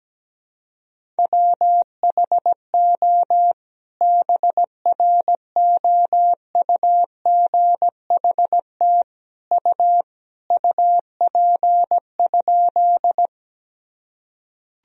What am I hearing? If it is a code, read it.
Morse code, heard as a keyed tone, about 17 wpm, 700 Hz. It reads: WHO BROUGHT U UP?
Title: Sentences from Top 500 Words and Abbreviations 17wpm